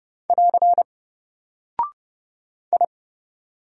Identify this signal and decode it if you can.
Morse code, heard as a keyed tone, about 30 wpm, 700 Hz. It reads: <AR> I
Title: Single Character (Letter:Number:Punctuation:Prosign) Alt 30wpm